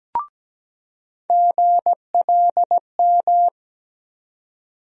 Morse code, heard as a keyed tone, about 17 wpm, 700 Hz. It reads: GLM
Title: USA Call Sign Postfixes - 3 Letters 17wpm